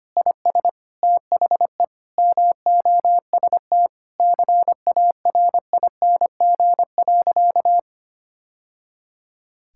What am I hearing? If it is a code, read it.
Morse code, heard as a keyed tone, about 25 wpm, 690 Hz. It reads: IS THE MOST CARING.
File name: practice / 25wpm / Sentences with 2-Word Phrases - 25wpm